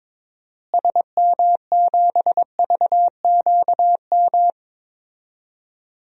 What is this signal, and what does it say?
Morse code, heard as a keyed tone, about 22 wpm, 700 Hz. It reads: SM7VQM
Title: DXCC-100 Call Signs - 6 Characters - Repeated 2x - 22wpm